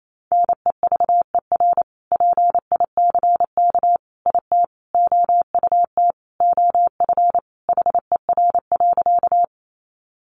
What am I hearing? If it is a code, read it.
Morse code, heard as a keyed tone, about 28 wpm, 700 Hz. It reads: NEVER PICK IT OUT OF HER.
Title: Sentences from Top 1000 Words 28wpm